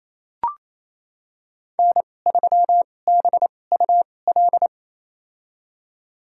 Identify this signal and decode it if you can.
Morse code, heard as a keyed tone, about 28 wpm, spaced as for 20 wpm, 700 Hz. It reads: N3BUL
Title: USA Call Signs - 5 Characters - 2x Character Spacing - 28wpm